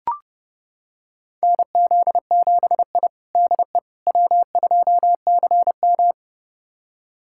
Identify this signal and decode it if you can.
Morse code, heard as a keyed tone, about 32 wpm, 700 Hz. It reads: NZ7I DE W2CM
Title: USA Callsign DE Callsign - 4 Characters - 30wpm